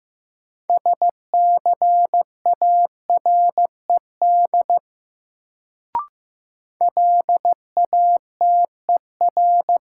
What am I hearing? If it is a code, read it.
Morse code, heard as a keyed tone, about 15 wpm, 700 Hz. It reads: SCARED LATER
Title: Sentences from Top 700 Words - New Word Drill - 15wpm